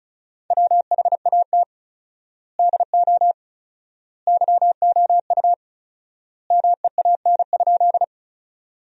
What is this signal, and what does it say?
Morse code, heard as a keyed tone, about 34 wpm, 700 Hz. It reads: WHAT DO YOU MEAN?
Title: 4x Word-Spacing - Sentences from Top 200 Words - 35wpm